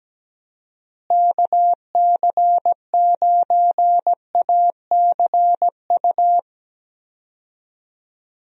Morse code, heard as a keyed tone, about 17 wpm, 700 Hz. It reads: KC9ACU
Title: USA Call Signs - 6 Characters - Repeated 3x - 17wpm